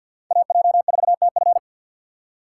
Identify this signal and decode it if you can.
Morse code, heard as a keyed tone, about 53 wpm, 700 Hz. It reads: AJ4TF
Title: K1USN SST QSO Practice - 50wpm